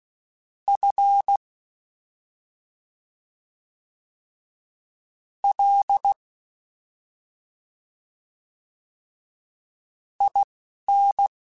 Morse code, heard as a keyed tone, about 16 wpm, spaced as for 3 wpm, 800 Hz. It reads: FLIN
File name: Special INGs-mp3s / Special INGs-mp3s / Slow ing @10